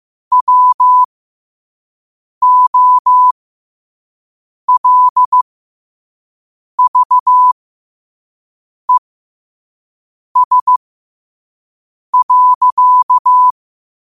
Morse code, heard as a keyed tone, about 15 wpm, spaced as for 5 wpm, 1000 Hz. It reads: WOLVES.
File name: qotdfiles / 5wpm